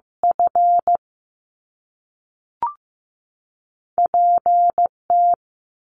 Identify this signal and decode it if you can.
Morse code, heard as a keyed tone, about 15 wpm, 700 Hz. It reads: F P T
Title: Top Two Letter Combinations in Words 15wpm